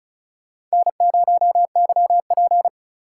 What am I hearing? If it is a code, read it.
Morse code, heard as a keyed tone, about 34 wpm, 700 Hz. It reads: N0YP
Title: USA Call Signs - 4 Characters - 3x Character Spacing - 35wpm